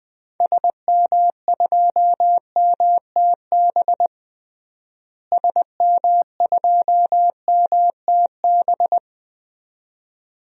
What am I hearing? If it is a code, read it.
Morse code, heard as a keyed tone, about 21 wpm, 700 Hz. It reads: SM2MTB SM2MTB
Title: DXCC-100 Call Signs - 6 Characters - Repeated 2x - 20wpm